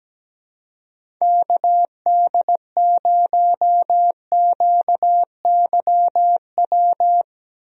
Morse code, heard as a keyed tone, about 17 wpm, 700 Hz. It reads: KD0QYW
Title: USA Call Signs - 6 Characters - Repeated 2x - 17wpm